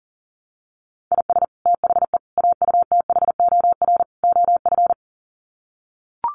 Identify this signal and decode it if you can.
Morse code, heard as a keyed tone, about 40 wpm, 700 Hz. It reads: IS THE AUTHOR OF
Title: Sets of 4 Words 40wpm